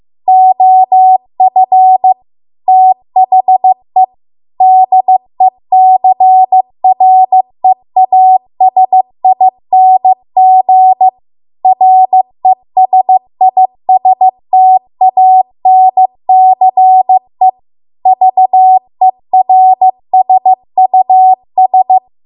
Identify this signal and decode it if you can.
Morse code, heard as a keyed tone, about 15 wpm, 750 Hz. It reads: OF THE DECREASING RESISTANCE VERSUS